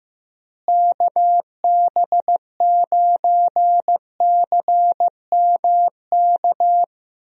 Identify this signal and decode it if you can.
Morse code, heard as a keyed tone, about 15 wpm, 700 Hz. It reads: KB9CMK
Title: USA Call Signs - 6 Characters - Repeated 2x - 15wpm